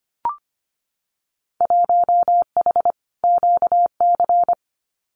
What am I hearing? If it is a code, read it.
Morse code, heard as a keyed tone, about 25 wpm, 700 Hz. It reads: 1H QC
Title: Winter Field Day Exchanges 25wpm